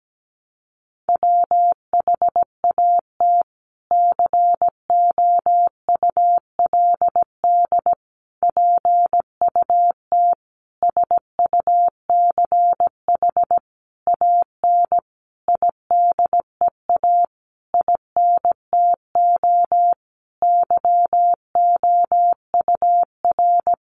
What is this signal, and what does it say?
Morse code, heard as a keyed tone, about 17 wpm, 700 Hz. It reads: WHAT COULD PUT SUCH AN IDEA INTO YOUR
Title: Sentences from Top 600 Words 17wpm